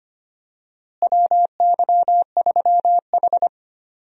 Morse code, heard as a keyed tone, about 25 wpm, 700 Hz. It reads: WY3H
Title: USA Call Signs - 4 Characters - Repeated 2x - 25wpm